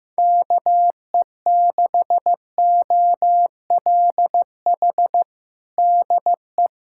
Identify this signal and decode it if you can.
Morse code, heard as a keyed tone, about 15 wpm, 700 Hz. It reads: KE6OLH DE
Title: USA Callsign DE Callsign - 6 Characters - 15wpm